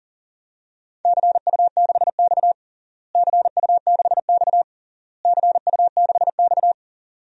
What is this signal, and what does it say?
Morse code, heard as a keyed tone, about 40 wpm, 700 Hz. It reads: CU6X CU6X CU6X
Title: DXCC-100 Call Signs - 4 Characters - Repeated 3x - 40wpm